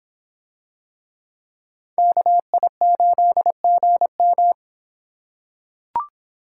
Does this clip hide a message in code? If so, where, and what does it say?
Yes: Morse code from 2.0 to 4.5 s, heard as a keyed tone, about 25 wpm, 700 Hz. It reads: KI8GM